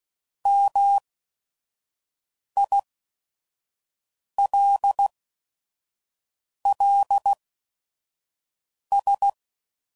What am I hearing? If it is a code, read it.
Morse code, heard as a keyed tone, about 16 wpm, spaced as for 5 wpm, 800 Hz. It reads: MILLS
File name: CW Code Groups @5WPM / fromSM's / Cities